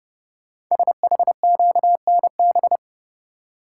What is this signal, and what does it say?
Morse code, heard as a keyed tone, about 30 wpm, 700 Hz. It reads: SHQNB
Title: Code Groups - 5 Letters - 30wpm